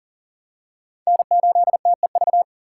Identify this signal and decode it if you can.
Morse code, heard as a keyed tone, about 37 wpm, 700 Hz. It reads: N8TEV
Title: USA Call Signs - 3x Character Spacing - 40wpm